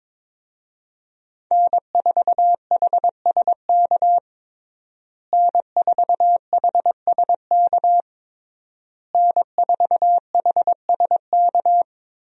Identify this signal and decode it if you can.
Morse code, heard as a keyed tone, about 21 wpm, 700 Hz. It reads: N4HSK N4HSK N4HSK
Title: USA Call Signs - 5 Characters - Repeated 3x - 22wpm